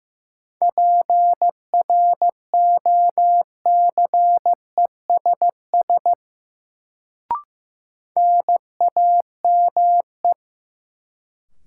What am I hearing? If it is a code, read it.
Morse code, heard as a keyed tone, about 15 wpm, 700 Hz. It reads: PROCESS NAME
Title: Sets of 3 Words - New Word Drill - 15wpm